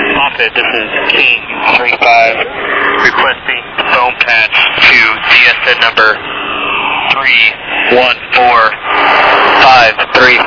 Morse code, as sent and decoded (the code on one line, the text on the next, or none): none